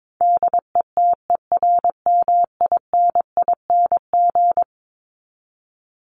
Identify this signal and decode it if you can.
Morse code, heard as a keyed tone, about 22 wpm, 700 Hz. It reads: DETERMINING
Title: ING Suffix - Top 100 Words with Base Words 22wpm